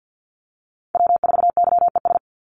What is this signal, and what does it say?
Morse code, heard as a keyed tone, about 50 wpm, 700 Hz. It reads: W4YES
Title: Call Signs Repeated 3x - 50wpm